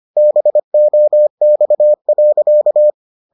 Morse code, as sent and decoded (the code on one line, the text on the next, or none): -... --- -..- .-.-.-
BOX.